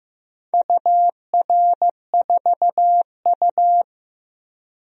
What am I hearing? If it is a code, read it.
Morse code, heard as a keyed tone, about 16 wpm, 700 Hz. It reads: UR4U
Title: DXCC-100 Call Signs - 4 Characters - Repeated 2x - 15wpm